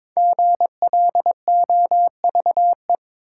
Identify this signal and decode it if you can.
Morse code, heard as a keyed tone, about 22 wpm, 700 Hz. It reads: GLOVE
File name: 5 Letter Words - First 3 Letters Given - 22wpm